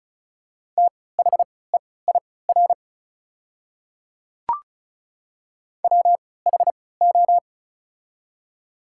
Morse code, heard as a keyed tone, about 36 wpm, spaced as for 20 wpm, 700 Hz. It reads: THEIR WHO
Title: 3x Character Spacing - Top 100 Words - 35wpm